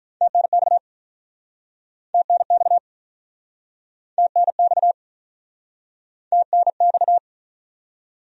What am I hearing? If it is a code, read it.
Morse code, heard as a keyed tone, about 49 wpm, 700 Hz. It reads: TNX TNX TNX TNX